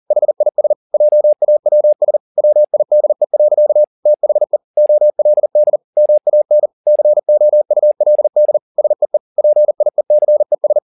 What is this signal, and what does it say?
Morse code, heard as a keyed tone, about 40 wpm, 600 Hz. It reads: HIS JAWS WIDE. THE OLD MAN COULD SEE PIECES
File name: the old man and the sea 2 40wpm